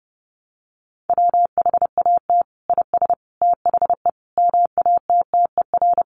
Morse code, heard as a keyed tone, about 30 wpm, 700 Hz. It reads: WHAT IS THE MATTER
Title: Sentences from Top 700 Words 30wpm